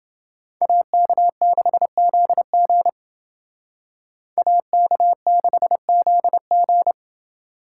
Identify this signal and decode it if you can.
Morse code, heard as a keyed tone, about 29 wpm, 700 Hz. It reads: AK6ZG AK6ZG